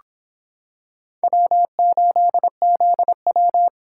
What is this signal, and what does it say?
Morse code, heard as a keyed tone, about 25 wpm, 700 Hz. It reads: W8ZW